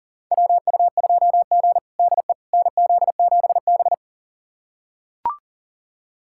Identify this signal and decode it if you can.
Morse code, heard as a keyed tone, about 37 wpm, 700 Hz. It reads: WU2G DE NZ7B
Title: USA Callsign DE Callsign - 4 Characters - 40wpm